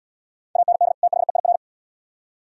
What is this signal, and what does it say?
Morse code, heard as a keyed tone, about 77 wpm, 690 Hz. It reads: AND THEIR